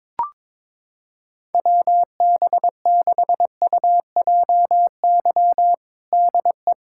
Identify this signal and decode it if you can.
Morse code, heard as a keyed tone, about 22 wpm, 700 Hz. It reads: WB6UJY DE
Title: USA Callsign DE Callsign 22wpm